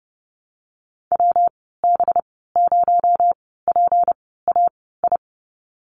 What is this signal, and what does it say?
Morse code, heard as a keyed tone, about 29 wpm, spaced as for 17 wpm, 700 Hz. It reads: WB0PAI
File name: Lesson 37g - Letter x - Call Signs 30wpm